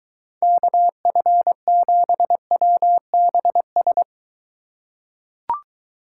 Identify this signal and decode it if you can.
Morse code, heard as a keyed tone, about 23 wpm, 700 Hz. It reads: KF7WBS